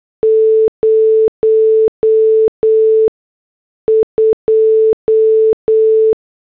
Morse code, heard as a keyed tone, about 8 wpm, spaced as for 6 wpm, 440 Hz. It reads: 02